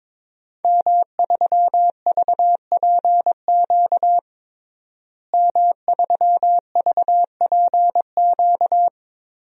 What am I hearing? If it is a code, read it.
Morse code, heard as a keyed tone, about 21 wpm, 700 Hz. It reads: M3VPQ M3VPQ
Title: DXCC-100 Call Signs - 5 Characters - Repeated 3x - 22wpm